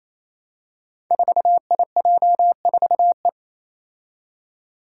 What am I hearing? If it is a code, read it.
Morse code, heard as a keyed tone, about 28 wpm, 700 Hz. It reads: 4IJ4E
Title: Five Challenging Letter-Number Combinations 28wpm